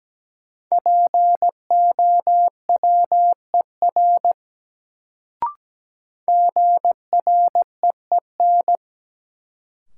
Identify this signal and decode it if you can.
Morse code, heard as a keyed tone, about 17 wpm, 700 Hz. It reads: POWER GREEN
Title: Sentences from Top 500 Words - New Word Drill - 17wpm